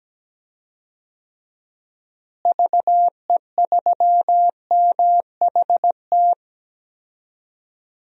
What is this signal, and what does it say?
Morse code, heard as a keyed tone, about 17 wpm, 700 Hz. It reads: VE3MHT
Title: DXCC-100 Call Signs - 6 Characters - Repeated 3x - 17wpm